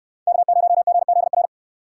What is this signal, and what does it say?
Morse code, heard as a keyed tone, about 65 wpm, 700 Hz. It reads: K0GZR